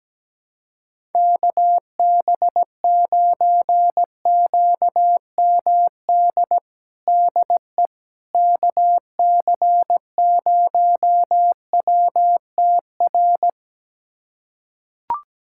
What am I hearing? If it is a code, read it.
Morse code, heard as a keyed tone, about 17 wpm, 700 Hz. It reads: KB9QMD DE KC0WTR
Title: USA Callsign DE Callsign 17wpm